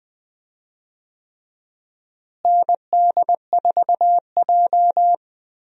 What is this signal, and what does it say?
Morse code, heard as a keyed tone, about 20 wpm, 700 Hz. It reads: ND4J
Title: USA Call Signs - 4 Characters - Repeated 3x - 20wpm